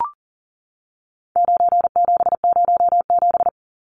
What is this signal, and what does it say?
Morse code, heard as a keyed tone, about 40 wpm, 700 Hz. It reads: 9707
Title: Numbers - 4-Digit 40wpm